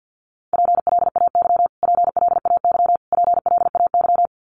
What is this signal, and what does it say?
Morse code, heard as a keyed tone, about 50 wpm, 700 Hz. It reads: PLAY PLAY PLAY
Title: Top 200 Words Repeated 50wpm